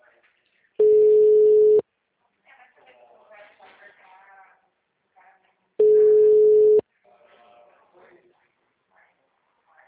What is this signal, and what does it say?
Ringback tone (Europe, 425 Hz), with speech.
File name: out- -2010- - - .1867